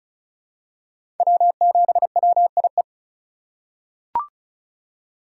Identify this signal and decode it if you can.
Morse code, heard as a keyed tone, about 36 wpm, 700 Hz. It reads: W7WIE
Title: USA Call Signs - 5 Characters - 2x Character Spacing - 35wpm